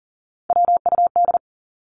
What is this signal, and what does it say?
Morse code, heard as a keyed tone, about 40 wpm, 700 Hz. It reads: WUD
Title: Abbreviations 40wpm